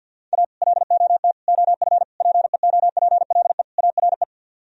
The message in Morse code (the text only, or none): A LOT OF PEOPLE ARE